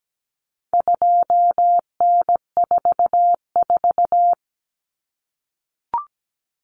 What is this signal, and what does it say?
Morse code, heard as a keyed tone, about 17 wpm, 700 Hz. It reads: 2N44
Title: Common Cut Numbers - 4-Digit 17wpm